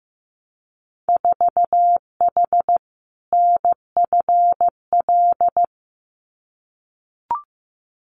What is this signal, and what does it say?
Morse code, heard as a keyed tone, about 15 wpm, 700 Hz. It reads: 4H NFL
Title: Winter Field Day Exchanges Encore 15wpm